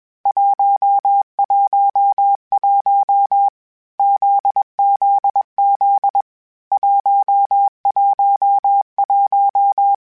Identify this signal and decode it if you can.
Morse code, heard as a keyed tone, about 21 wpm, 800 Hz. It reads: 111 ZZZ 111
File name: Basic Letter Groups Collective7/JZ17@20